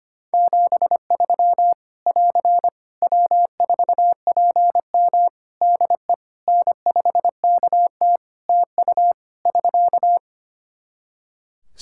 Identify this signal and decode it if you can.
Morse code, heard as a keyed tone, about 25 wpm, 700 Hz. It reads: 73 <AR> W4PM DE N5KT TU <SK>
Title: K1USN SST QSO Practice 25wpm